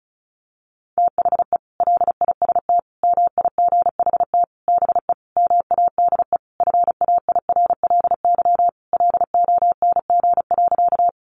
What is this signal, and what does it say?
Morse code, heard as a keyed tone, about 34 wpm, 700 Hz. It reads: THE LIST MIGHT BE MADE FAIRLY LONG.
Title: Sentences from Top 1100 Words 35wpm